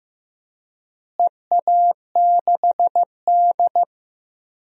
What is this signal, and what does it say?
Morse code, heard as a keyed tone, about 15 wpm, 700 Hz. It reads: EA6D
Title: DXCC-100 Call Signs - 4 Characters - Repeated 3x - 15wpm